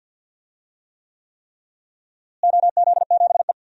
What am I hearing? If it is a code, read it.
Morse code, heard as a keyed tone, about 52 wpm, 700 Hz. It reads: OZ7E
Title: DXCC-100 Call Signs - 4 Characters - Repeated 3x - 50wpm